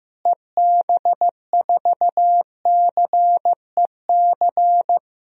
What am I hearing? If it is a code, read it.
Morse code, heard as a keyed tone, about 15 wpm, 700 Hz. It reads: EB4CEC
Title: DXCC-100 Call Signs - 6 Characters - Repeated 3x - 15wpm